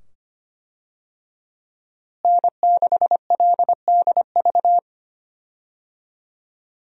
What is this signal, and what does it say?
Morse code, heard as a keyed tone, about 25 wpm, 700 Hz. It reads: N6LDV
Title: USA Call Signs - 5 Characters - Repeated 2x - 25wpm